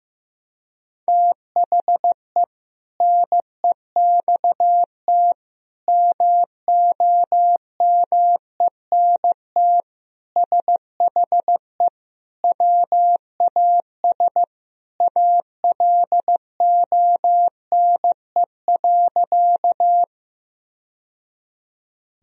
Morse code, heard as a keyed tone, about 15 wpm, 700 Hz. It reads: THE NEXT MOMENT SHE WAS ALONE.
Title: Sentences from Top 1000 Words - Review Part 1 - 15wpm